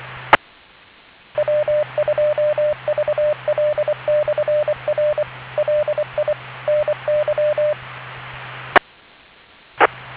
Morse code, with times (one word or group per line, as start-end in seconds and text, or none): 0.3-0.4 E
1.4-5.2 W2VL/R
5.6-6.3 LI
6.7-7.7 NY
8.7-8.8 E
9.8-9.9 E